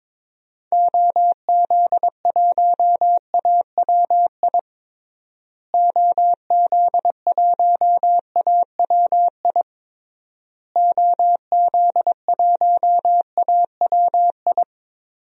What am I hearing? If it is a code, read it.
Morse code, heard as a keyed tone, about 22 wpm, 700 Hz. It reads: OZ1AWI OZ1AWI OZ1AWI
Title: DXCC-100 Call Signs - 6 Characters - Repeated 3x - 22wpm